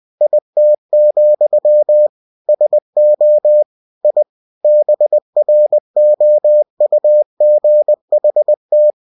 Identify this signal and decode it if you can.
Morse code, heard as a keyed tone, about 20 wpm, 600 Hz. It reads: IT, SO I BROUGHT
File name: cw-books / winnie-the-pooh / winnie-20wpm-page-0098